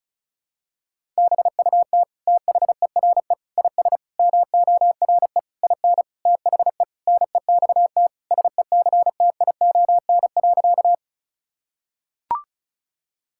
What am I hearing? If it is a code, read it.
Morse code, heard as a keyed tone, about 36 wpm, 700 Hz. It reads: BUT THERE IS MORE IN THE NEXT SECTION.
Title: Sentences from Top 1000 Words - Review Part 5 - 35wpm